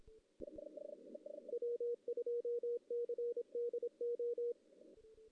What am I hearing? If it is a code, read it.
Morse code, heard as a keyed tone, about 24 wpm, 480 Hz. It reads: W2CDO
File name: 2016 JAN CW NAQP / 2016 Jan naqp k5ot - 0-2